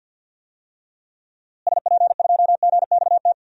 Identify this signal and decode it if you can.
Morse code, heard as a keyed tone, about 53 wpm, 700 Hz. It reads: SP2GXT
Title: DXCC-100 Call Signs - 6 Characters - Repeated 2x - 50wpm